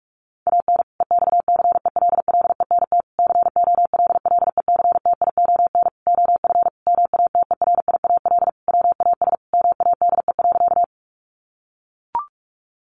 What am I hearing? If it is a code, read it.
Morse code, heard as a keyed tone, about 44 wpm, 700 Hz. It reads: AN EXCELLENT COLLECTION OF MATERIAL WAS MADE.
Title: Sentences from Top 1200 Words 45wpm